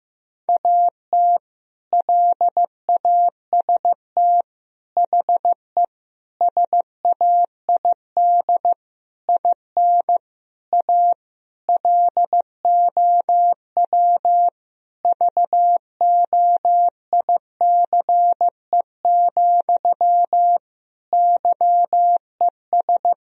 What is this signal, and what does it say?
Morse code, heard as a keyed tone, about 16 wpm, 700 Hz. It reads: AT LAST HE SAID IN A LOW VOICE, YES
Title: Sentences from Top 1000 Words - Review Part 1 - 15wpm